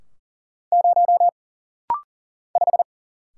Morse code, heard as a keyed tone, about 39 wpm, 700 Hz. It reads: 0 5
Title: Single Letter-Number - Rapid-Fire 40wpm